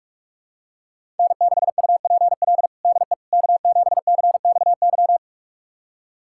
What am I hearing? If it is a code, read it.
Morse code, heard as a keyed tone, about 47 wpm, 700 Hz. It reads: N6UPL DE K7CXY
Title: USA Callsign DE Callsign - 5 Characters - 45wpm